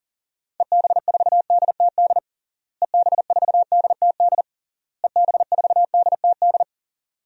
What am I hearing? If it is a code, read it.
Morse code, heard as a keyed tone, about 40 wpm, 700 Hz. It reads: EB4DTD EB4DTD EB4DTD
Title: DXCC-100 Call Signs - 6 Characters - Repeated 3x - 40wpm